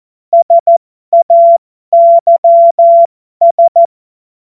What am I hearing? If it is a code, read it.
Morse code, heard as a keyed tone, about 14 wpm, spaced as for 12 wpm, 680 Hz. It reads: SAYS
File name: Morsekurs / Witze